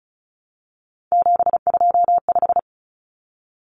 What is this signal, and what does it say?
Morse code, heard as a keyed tone, about 34 wpm, 700 Hz. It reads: 725
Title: Common Cut Numbers - 3-Digit 35wpm